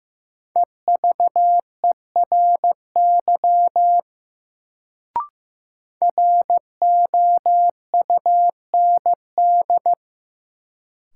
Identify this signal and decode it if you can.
Morse code, heard as a keyed tone, about 16 wpm, 700 Hz. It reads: EVERY ROUND
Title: Sentences from Top 200 Words - New Word Drill - 15wpm